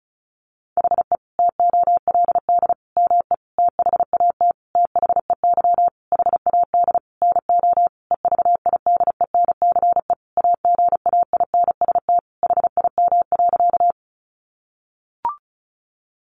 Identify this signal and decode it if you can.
Morse code, heard as a keyed tone, about 34 wpm, 700 Hz. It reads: HE TOLD ME THAT THEY HAD NO EVIDENCE AGAINST HIM.
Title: Sentences from Top 1300 Words 35wpm